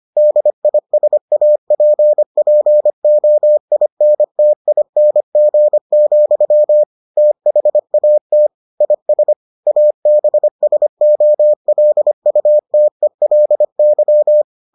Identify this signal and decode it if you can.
Morse code, heard as a keyed tone, about 25 wpm, 600 Hz. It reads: DISAPPOINTING, THAT IS ABSOLUTELY